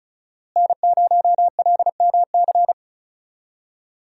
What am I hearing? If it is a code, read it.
Morse code, heard as a keyed tone, about 36 wpm, 700 Hz. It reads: N0LMC
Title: USA Call Signs - 5 Characters - Repeated 3x - 35wpm